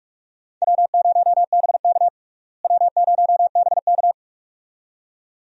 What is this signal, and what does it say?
Morse code, heard as a keyed tone, about 44 wpm, 700 Hz. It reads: W0BK W0BK
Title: USA Call Signs - 4 Characters - Repeated 2x - 45wpm